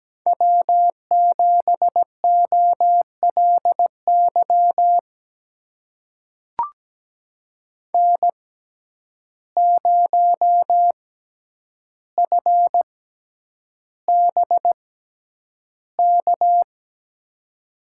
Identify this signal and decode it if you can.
Morse code, heard as a keyed tone, about 17 wpm, spaced as for 9 wpm, 700 Hz. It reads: W7OLY N0FBK
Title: USA Call Signs - 5 Characters - 6x Character Spacing - 17wpm